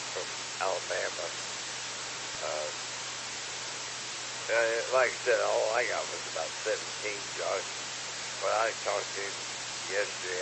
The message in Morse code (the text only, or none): none